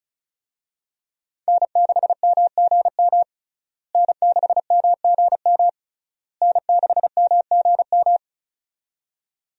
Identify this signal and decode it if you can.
Morse code, heard as a keyed tone, about 36 wpm, 700 Hz. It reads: N6MGM N6MGM N6MGM
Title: USA Call Signs - 5 Characters - Repeated 3x - 35wpm